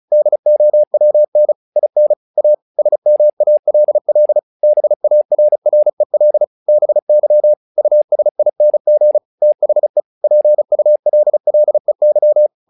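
Morse code, heard as a keyed tone, about 34 wpm, 600 Hz. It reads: DOWN IN A SMALL BARREL BY USING THE PULLEY
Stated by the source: Morse practice text "Tower Trouble" sent at 35 WPM, from the SKCC - Straight Key Century Club learning center audio files.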